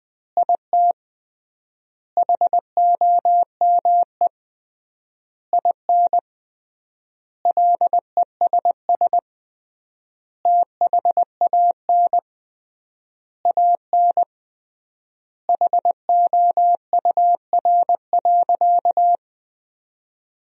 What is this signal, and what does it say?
Morse code, heard as a keyed tone, about 21 wpm, 700 Hz. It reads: IT HOME IN LESS THAN AN HOUR.
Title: 3x Word-Spacing - Sentences from Top 500 Words - 20wpm